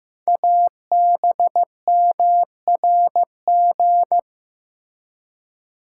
Morse code, heard as a keyed tone, about 15 wpm, 700 Hz. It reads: ABMRG
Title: Code Groups - 5 Letters - 15wpm